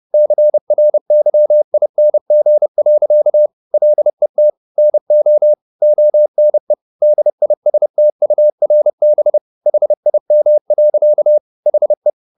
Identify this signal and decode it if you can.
Morse code, heard as a keyed tone, about 30 wpm, 600 Hz. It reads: CRYING. LET NO ONE DISTURB HIM. HE